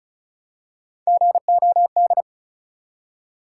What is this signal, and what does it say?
Morse code, heard as a keyed tone, about 36 wpm, 700 Hz. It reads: GOD
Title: Three Letter Words - 35wpm